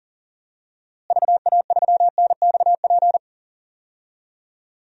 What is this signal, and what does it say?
Morse code, heard as a keyed tone, about 37 wpm, 700 Hz. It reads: VA3NXP
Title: DXCC-100 Call Signs - 6 Characters - Repeated 2x - 40wpm